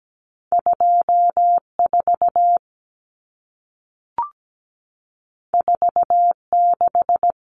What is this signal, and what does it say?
Morse code, heard as a keyed tone, about 17 wpm, 700 Hz. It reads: 24 46
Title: Numbers - 2-Digit 17wpm